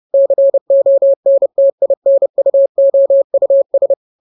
Morse code, heard as a keyed tone, about 30 wpm, 550 Hz. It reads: CONTINUOUS
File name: jokes / joke 5 30 0000